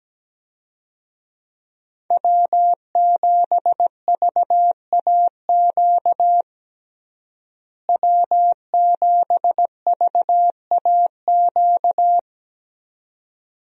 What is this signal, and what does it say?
Morse code, heard as a keyed tone, about 17 wpm, 700 Hz. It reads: W7VAQ W7VAQ
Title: USA Call Signs - 5 Characters - Repeated 2x - 17wpm